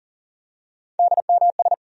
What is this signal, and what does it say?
Morse code, heard as a keyed tone, about 40 wpm, 700 Hz. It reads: DMS
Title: USA Call Sign Postfixes - 3 Letters 40wpm